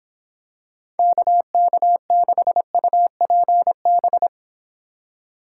Morse code, heard as a keyed tone, about 26 wpm, 700 Hz. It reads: KK6UPB